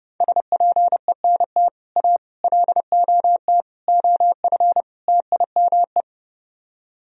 Morse code, heard as a keyed tone, about 30 wpm, 700 Hz. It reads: SPENT A LOT OF TIME